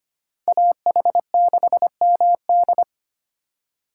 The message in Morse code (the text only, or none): AH6MD